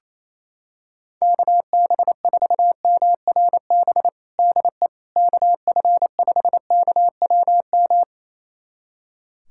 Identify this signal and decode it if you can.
Morse code, heard as a keyed tone, about 28 wpm, 700 Hz. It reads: KB4MRB DE KF5KWM
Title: USA Callsign DE Callsign 28wpm